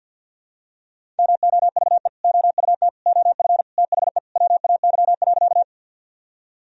Morse code, heard as a keyed tone, about 49 wpm, 700 Hz. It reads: MOVE OUT OF THE WAY.
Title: Sentences from Top 1000 Words - Review Part 1 - 50wpm